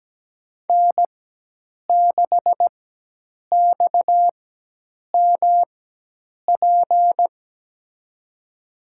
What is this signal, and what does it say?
Morse code, heard as a keyed tone, about 17 wpm, spaced as for 8 wpm, 700 Hz. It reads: N6XMP
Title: USA Call Signs - 5 Characters - 4x Character Spacing - 17wpm